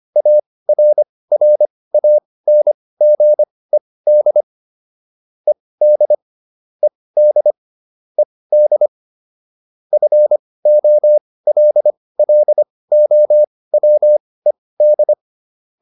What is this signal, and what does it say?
Morse code, heard as a keyed tone, about 25 wpm, spaced as for 18 wpm, 600 Hz. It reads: ARRANGED ED ED ED FOLLOWED